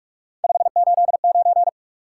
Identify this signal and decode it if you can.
Morse code, heard as a keyed tone, about 56 wpm, 700 Hz. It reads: 589
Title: Speed-Racing Top 180 QSO Elements 45wpm